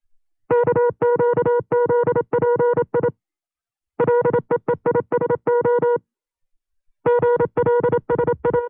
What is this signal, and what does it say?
Morse code, heard as a keyed tone, about 27 wpm, 480 Hz. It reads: KQZPI LEEISO GLSI
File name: CW-practice/25WPM